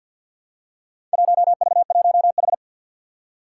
Morse code, heard as a keyed tone, about 50 wpm, 700 Hz. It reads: 1V1H
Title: Four Challenging Letter-Number Combinations 50wpm